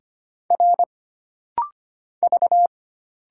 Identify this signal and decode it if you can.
Morse code, heard as a keyed tone, about 25 wpm, 700 Hz. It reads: R V
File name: Single Character (Letter:Number:Punctuation:Prosign) - Rapid-Fire - 25wpm